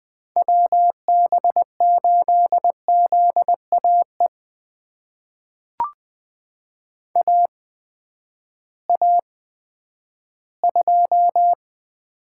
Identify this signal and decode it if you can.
Morse code, heard as a keyed tone, about 20 wpm, spaced as for 11 wpm, 700 Hz. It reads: WB8ZAE AA2
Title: USA Call Signs - 8x Character Spacing - 20wpm